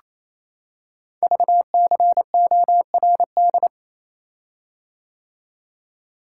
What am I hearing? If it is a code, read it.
Morse code, heard as a keyed tone, about 27 wpm, 700 Hz. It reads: VCORD
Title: Code Groups - 5 Letters - 28wpm